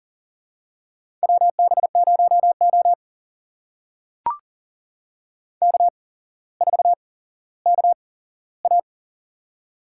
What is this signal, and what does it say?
Morse code, heard as a keyed tone, about 43 wpm, spaced as for 17 wpm, 700 Hz. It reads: WB0O K4KA
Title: USA Call Signs - 4 Characters - 8x Character Spacing - 40wpm